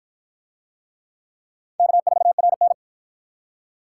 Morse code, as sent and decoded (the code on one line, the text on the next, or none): -.- ....- .-. -.
K4RN